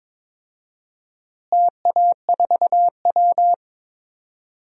Morse code, heard as a keyed tone, about 22 wpm, 700 Hz. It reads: TA4W
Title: DXCC-100 Call Signs - 4 Characters - Repeated 3x - 22wpm